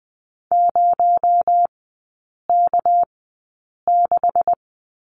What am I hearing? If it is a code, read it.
Morse code, heard as a keyed tone, about 20 wpm, spaced as for 8 wpm, 700 Hz. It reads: 0K6
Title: 218 Lesson 35e - Number 0 - Three Characters 20wpm